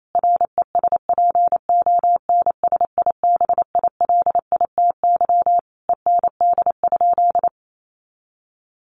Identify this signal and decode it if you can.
Morse code, heard as a keyed tone, about 28 wpm, 700 Hz. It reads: RESPONSIBILITY END?
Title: Sentences from Top 1200 Words 28wpm